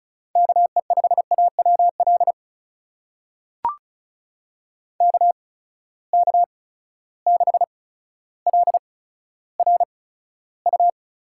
Morse code, heard as a keyed tone, about 36 wpm, spaced as for 15 wpm, 700 Hz. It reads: KE5AWL KK6LRU
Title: USA Call Signs - 8x Character Spacing - 35wpm